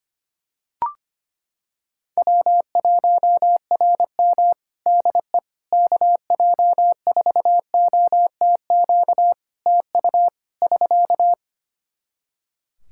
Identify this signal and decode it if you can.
Morse code, heard as a keyed tone, about 25 wpm, 700 Hz. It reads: W1RM DE KJ4OTQ TU <SK>
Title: Practice QSO - Exchange 4 - Ending - 25wpm